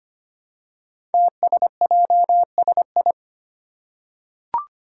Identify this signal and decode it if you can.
Morse code, heard as a keyed tone, about 25 wpm, 700 Hz. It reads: TSJSI
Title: Five Challenging Letter-Number Combinations 25wpm